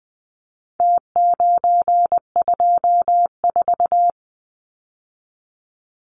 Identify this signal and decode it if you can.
Morse code, heard as a keyed tone, about 20 wpm, 700 Hz. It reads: T924
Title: Numbers - Contest Style - 4-Digit 20wpm